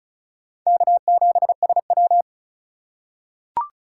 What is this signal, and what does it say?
Morse code, heard as a keyed tone, about 36 wpm, 700 Hz. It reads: K7SW
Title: USA Call Signs - 4x Character Spacing - 35wpm